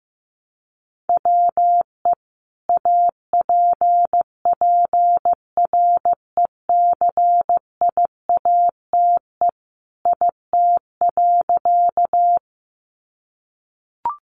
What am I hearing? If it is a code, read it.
Morse code, heard as a keyed tone, about 15 wpm, 700 Hz. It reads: WE APPRECIATE IT.
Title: Sentences from Top 1300 Words 15wpm